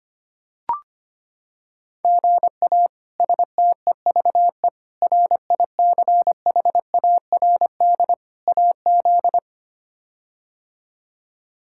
Morse code, heard as a keyed tone, about 25 wpm, 700 Hz. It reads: GA STEVE RICHARD AZ
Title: K1USN SST QSO Practice - 25wpm